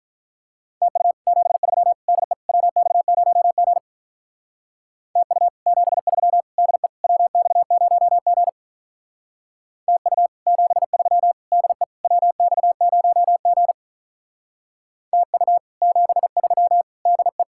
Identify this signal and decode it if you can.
Morse code, heard as a keyed tone, about 43 wpm, 700 Hz. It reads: TU 73 DE WX0G TU 73 DE WX0G TU 73 DE WX0G TU 73 DE
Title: Speed-Racing Common QSO Phrases - 35wpm